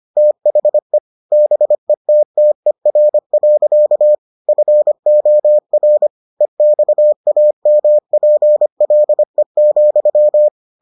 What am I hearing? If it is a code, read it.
Morse code, heard as a keyed tone, about 25 wpm, 600 Hz. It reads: THE BETTER. FOR EXAMPLE,